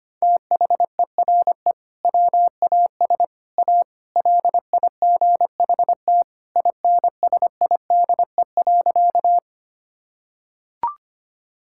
Morse code, heard as a keyed tone, about 25 wpm, 700 Hz. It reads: THERE WAS A LIGHT INSIDE.
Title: Sentences from Top 1000 Words - Review Part 6 - 25wpm